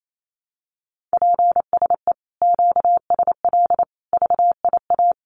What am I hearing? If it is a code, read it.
Morse code, heard as a keyed tone, about 28 wpm, 700 Hz. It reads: PSE QSL VIA
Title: Common QSO Phrases Encore 28wpm